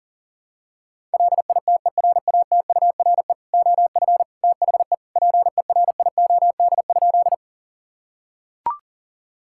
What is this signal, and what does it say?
Morse code, heard as a keyed tone, about 37 wpm, 700 Hz. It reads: LITERATURE OF THE PERIOD?
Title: Sentences from Top 1000 Words - Review Part 5 - 40wpm